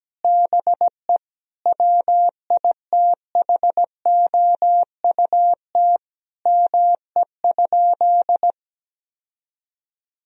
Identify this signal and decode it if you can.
Morse code, heard as a keyed tone, about 17 wpm, 700 Hz. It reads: BE WITHOUT ME?
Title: Sentences from Top 1000 Words - Review Part 3 - 17wpm